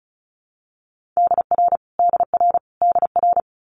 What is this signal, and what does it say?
Morse code, heard as a keyed tone, about 36 wpm, 700 Hz. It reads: DR DR DR
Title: Multi-letter Phonemes Repeated 3x 35wpm